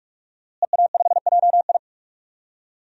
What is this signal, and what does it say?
Morse code, heard as a keyed tone, about 43 wpm, 700 Hz. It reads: EA5JI
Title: DXCC-100 Call Signs - 5 Characters - Repeated 3x - 45wpm